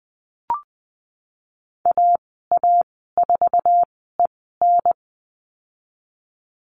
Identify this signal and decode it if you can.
Morse code, heard as a keyed tone, about 20 wpm, spaced as for 15 wpm, 700 Hz. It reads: AA4EN
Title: 055 Lesson 10g - Letter r - Call Signs 20wpm